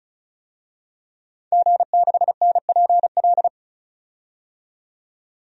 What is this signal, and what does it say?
Morse code, heard as a keyed tone, about 36 wpm, 700 Hz. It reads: G6NPL
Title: DXCC-100 Call Signs - 5 Characters - 35wpm